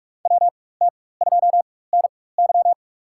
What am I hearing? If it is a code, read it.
Morse code, heard as a keyed tone, about 41 wpm, spaced as for 20 wpm, 700 Hz. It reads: WT2NY
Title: USA Call Signs - 5 Characters - 4x Character Spacing - 45wpm